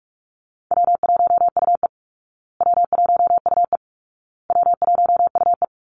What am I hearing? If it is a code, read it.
Morse code, heard as a keyed tone, about 44 wpm, 700 Hz. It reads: W1UE W1UE W1UE
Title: Easy Call Signs Repeated 3x - 45wpm